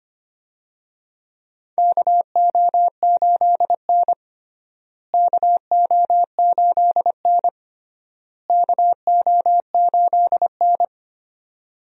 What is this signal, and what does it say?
Morse code, heard as a keyed tone, about 25 wpm, 700 Hz. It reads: KO8N KO8N KO8N
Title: USA Call Signs - 4 Characters - Repeated 3x - 25wpm